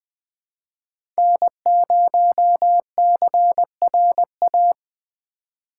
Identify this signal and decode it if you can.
Morse code, heard as a keyed tone, about 20 wpm, 700 Hz. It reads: N0CRA
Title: USA Call Signs - 2x Character Spacing - 20wpm